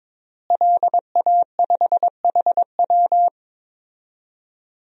Morse code, heard as a keyed tone, about 22 wpm, 700 Hz. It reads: LA5HW
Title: DXCC-100 Call Signs - 5 Characters - Repeated 3x - 22wpm